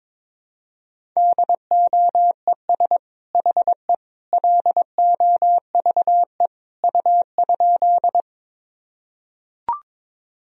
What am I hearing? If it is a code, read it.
Morse code, heard as a keyed tone, about 22 wpm, 700 Hz. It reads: DOES HE LOVE U?
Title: Sentences from Top 400 Words and Abbreviations 22wpm